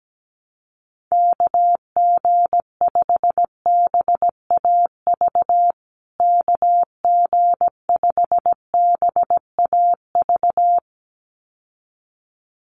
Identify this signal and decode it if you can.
Morse code, heard as a keyed tone, about 17 wpm, 700 Hz. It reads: KG5BAV KG5BAV
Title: USA Call Signs Repeated 2x 17wpm